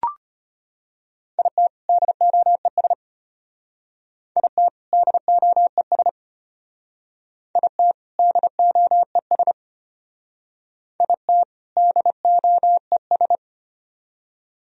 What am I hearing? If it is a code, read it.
Morse code, heard as a keyed tone, about 34 wpm, 700 Hz. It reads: IT DOES IT DOES IT DOES IT DOES